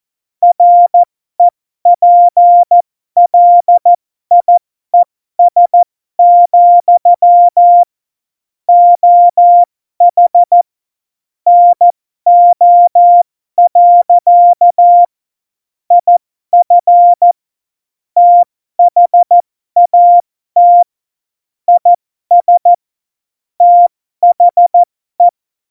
Morse code, heard as a keyed tone, about 14 wpm, spaced as for 12 wpm, 700 Hz. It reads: REPLIES, OH NO. IF THAT IS THE
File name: Morsekurs / Witze